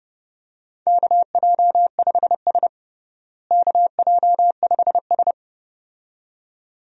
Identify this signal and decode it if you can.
Morse code, heard as a keyed tone, about 29 wpm, 700 Hz. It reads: KJ5S KJ5S
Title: USA Call Signs - 4 Characters - Repeated 2x - 30wpm